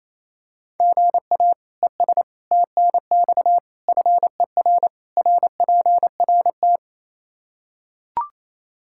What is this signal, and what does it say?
Morse code, heard as a keyed tone, about 28 wpm, 700 Hz. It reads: GA ES TNX FER RPRT
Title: Practice QSO - Exchange 1 - Beginning - 28wpm